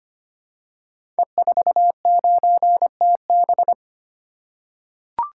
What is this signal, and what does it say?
Morse code, heard as a keyed tone, about 25 wpm, 700 Hz. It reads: E49TB
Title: Code Groups - 5 Letter:Numbers - 25wpm